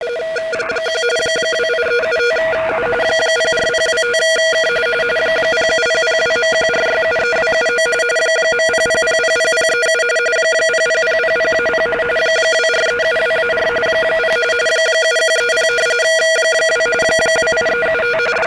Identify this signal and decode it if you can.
RTTY, about 45.45 baud. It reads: 1 28375U 04025K 07002.14673279 .00000037 00000-0 24372-4 0 7130 2 28375
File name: SignalSamples / rtty 3596MHz